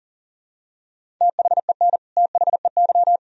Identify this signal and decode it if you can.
Morse code, heard as a keyed tone, about 40 wpm, 700 Hz. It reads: THEN THEY
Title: Sentences from Top 1000 Words - Review Part 1 - 40wpm